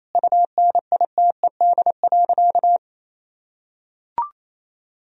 Morse code, heard as a keyed tone, about 28 wpm, 700 Hz. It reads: UNITED.
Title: Sentences from Top 1000 Words - Review Part 4 - 28wpm